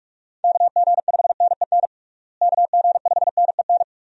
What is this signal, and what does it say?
Morse code, heard as a keyed tone, about 42 wpm, 700 Hz. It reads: KG5NEN KG5NEN
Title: USA Call Signs - 6 Characters - Repeated 2x - 45wpm